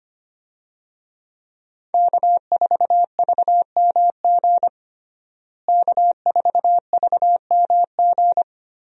Morse code, heard as a keyed tone, about 25 wpm, 700 Hz. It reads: K4VMG K4VMG
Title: USA Call Signs - 5 Characters - Repeated 2x - 25wpm